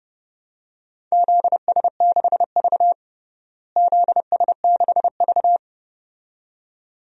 Morse code, heard as a keyed tone, about 32 wpm, 700 Hz. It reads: ZS6V ZS6V
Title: DXCC-100 Call Signs - 4 Characters - Repeated 2x - 30wpm